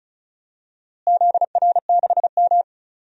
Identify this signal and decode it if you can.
Morse code, heard as a keyed tone, about 36 wpm, 700 Hz. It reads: ZR6M
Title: DXCC-100 Call Signs - 4 Characters - Repeated 2x - 35wpm